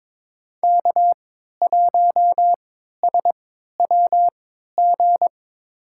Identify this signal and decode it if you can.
Morse code, heard as a keyed tone, about 22 wpm, spaced as for 12 wpm, 700 Hz. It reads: K1SWG
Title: USA Call Signs - 3x Character Spacing - 22wpm